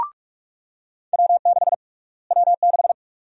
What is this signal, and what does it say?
Morse code, heard as a keyed tone, about 46 wpm, 700 Hz. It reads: W6 W6
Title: USA Call Sign Prefixes - Repeated 2x - 45wpm